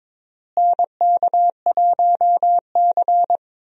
Morse code, heard as a keyed tone, about 22 wpm, 700 Hz. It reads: NK1C
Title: USA Call Signs - 4 Characters - Repeated 2x - 22wpm